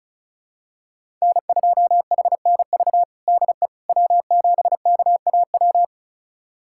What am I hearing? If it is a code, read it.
Morse code, heard as a keyed tone, about 34 wpm, 700 Hz. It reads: N2HNV DE W7KAW
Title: USA Callsign DE Callsign - 5 Characters - 35wpm